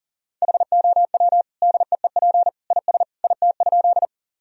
Morse code, heard as a keyed tone, about 37 wpm, 700 Hz. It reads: HOW DEEP IS IT?
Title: Sentences from Top 1000 Words - Review Part 3 - 40wpm